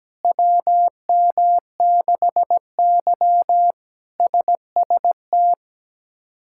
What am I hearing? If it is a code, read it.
Morse code, heard as a keyed tone, about 17 wpm, 700 Hz. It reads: WM6Y SST
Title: K1USN SST QSO Practice - 17wpm